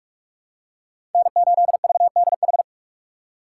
Morse code, heard as a keyed tone, about 43 wpm, 690 Hz. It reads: N8VDH
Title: USA Call Signs - 5 Characters - 2x Character Spacing - 45wpm